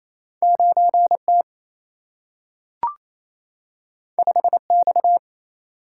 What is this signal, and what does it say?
Morse code, heard as a keyed tone, about 28 wpm, 700 Hz. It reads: 9T 5X
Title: Single-Letter Single-Number 28wpm